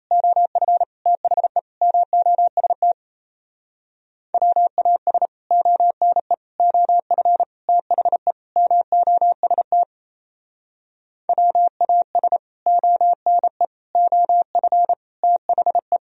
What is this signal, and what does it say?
Morse code, heard as a keyed tone, about 37 wpm, 700 Hz. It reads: OF THE MOST WAS ONE OF THE MOST WAS ONE OF THE